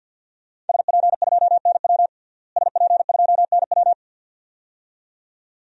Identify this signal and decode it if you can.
Morse code, heard as a keyed tone, about 52 wpm, 700 Hz. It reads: SP2NW SP2NW
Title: DXCC-100 Call Signs - 5 Characters - Repeated 2x - 50wpm